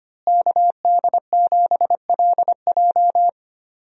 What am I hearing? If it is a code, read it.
Morse code, heard as a keyed tone, about 25 wpm, 700 Hz. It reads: KD7LJ
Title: USA Call Signs - 5 Characters - Repeated 3x - 25wpm